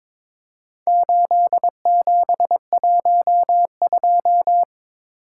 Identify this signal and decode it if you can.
Morse code, heard as a keyed tone, about 23 wpm, 700 Hz. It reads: 8712
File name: practice / 22wpm / Four Letter-Number Combinations - 22wpm